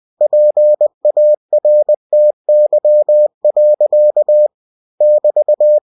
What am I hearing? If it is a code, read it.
Morse code, heard as a keyed tone, about 20 wpm, 600 Hz. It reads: PARTY. <BT>
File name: cw-books / winnie-the-pooh / winnie-20wpm-page-0102